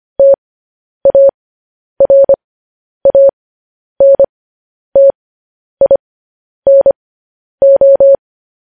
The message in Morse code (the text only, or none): TARANTINO